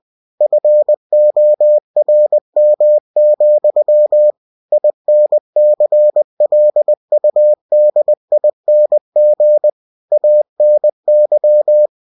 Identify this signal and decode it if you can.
Morse code, heard as a keyed tone, about 20 wpm, 600 Hz. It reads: FORM, INCLUDING ANY
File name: cw-books / winnie-the-pooh / winnie-20wpm-page-0116